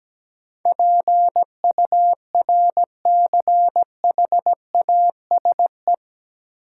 Morse code, heard as a keyed tone, about 17 wpm, 700 Hz. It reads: PURCHASE
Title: Sentences from Top 800 Words - New Word Drill - 17wpm